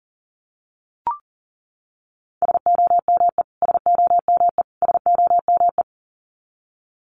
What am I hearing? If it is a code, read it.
Morse code, heard as a keyed tone, about 40 wpm, 700 Hz. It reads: SOME SOME SOME
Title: Top 300 Words Repeated 40wpm